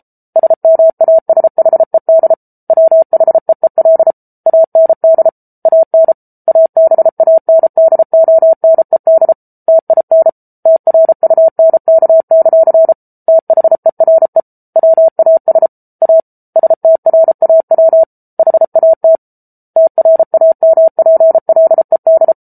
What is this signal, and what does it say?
Morse code, heard as a keyed tone, about 34 wpm, 650 Hz. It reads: SMASHED WHEEL AND AN ABANDONED TIN TRUNK; THERE WAS A STRAW HAT TRAMPLED